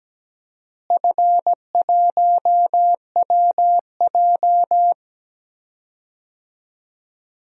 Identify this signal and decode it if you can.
Morse code, heard as a keyed tone, about 17 wpm, 700 Hz. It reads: F1WJ
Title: DXCC-100 Call Signs - 4 Characters - Repeated 3x - 17wpm